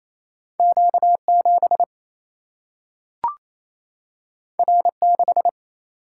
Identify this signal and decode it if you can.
Morse code, heard as a keyed tone, about 28 wpm, 700 Hz. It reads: Q 7 R 6
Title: Single-Letter Single-Number 28wpm